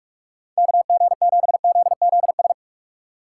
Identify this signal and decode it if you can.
Morse code, heard as a keyed tone, about 42 wpm, 700 Hz. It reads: KG7ZZS